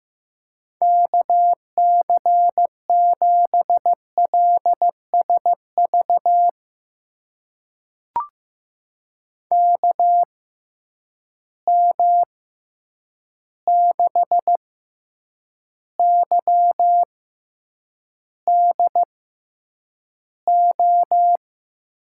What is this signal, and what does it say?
Morse code, heard as a keyed tone, about 15 wpm, spaced as for 8 wpm, 700 Hz. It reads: KC7LSV KM6YDO
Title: USA Call Signs - 6 Characters - 6x Character Spacing - 15wpm